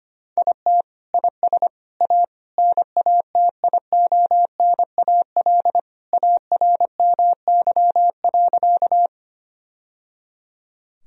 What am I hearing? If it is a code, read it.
Morse code, heard as a keyed tone, about 25 wpm, 700 Hz. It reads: IT IS A NATIONAL ARMY.
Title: Sentences from Top 1000 Words - Review Part 4 - 25wpm